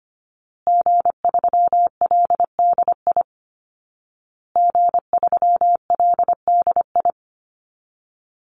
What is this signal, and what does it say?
Morse code, heard as a keyed tone, about 25 wpm, 700 Hz. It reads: G3LDI G3LDI
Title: Call Signs Repeated 2x - 25wpm